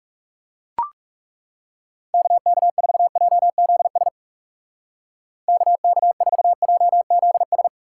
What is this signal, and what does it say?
Morse code, heard as a keyed tone, about 39 wpm, 700 Hz. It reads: KK4JZS KK4JZS